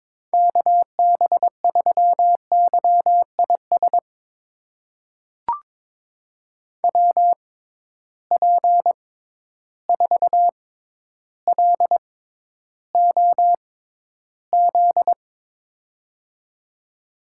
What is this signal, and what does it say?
Morse code, heard as a keyed tone, about 22 wpm, spaced as for 11 wpm, 700 Hz. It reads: KB3YIS WP4LOZ
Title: USA Call Signs - 6 Characters - 6x Character Spacing - 22wpm